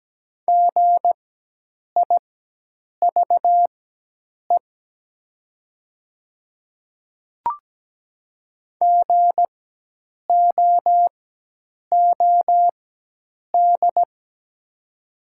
Morse code, heard as a keyed tone, about 17 wpm, spaced as for 8 wpm, 700 Hz. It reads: GIVE GOOD
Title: 4x Character Spacing - Top 300 Words - 17wpm